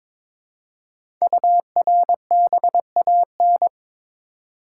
Morse code, heard as a keyed tone, about 23 wpm, 700 Hz. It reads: URBAN
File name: 5 Letter Words - First 3 Letters Given - 22wpm